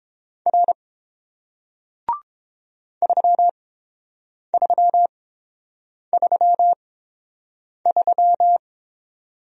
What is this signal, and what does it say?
Morse code, heard as a keyed tone, about 30 wpm, spaced as for 7 wpm, 700 Hz. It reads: R 3333